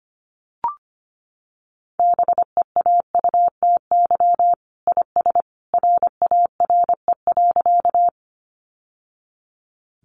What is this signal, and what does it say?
Morse code, heard as a keyed tone, about 25 wpm, 700 Hz. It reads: BEAUTY IS RARE.
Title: Sentences from Top 1100 Words 25wpm